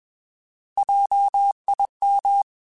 Morse code, heard as a keyed tone, about 21 wpm, 800 Hz. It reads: JIM
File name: Alphabetical / Alphabetical mp3s / Js @20